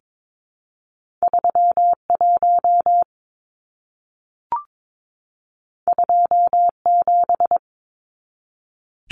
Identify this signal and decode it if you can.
Morse code, heard as a keyed tone, about 22 wpm, 700 Hz. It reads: 31 27
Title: Numbers - 2-Digit 22wpm